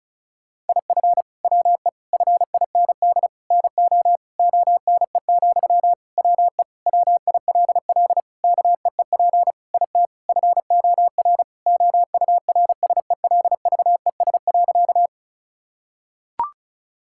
Morse code, heard as a keyed tone, about 36 wpm, 700 Hz. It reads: IF WE FIND NO ONE, WE WILL KEEP IT FOR OURSELVES.
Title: Sentences from Top 1600 Words 35wpm